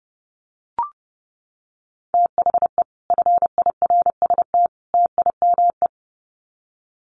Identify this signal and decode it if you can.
Morse code, heard as a keyed tone, about 30 wpm, 700 Hz. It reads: THE FIRST TIME
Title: Sets of 3 Words 30wpm